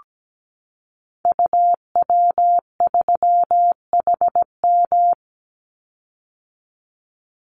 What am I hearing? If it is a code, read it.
Morse code, heard as a keyed tone, about 17 wpm, 700 Hz. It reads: UW3HM
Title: Call Signs - 17wpm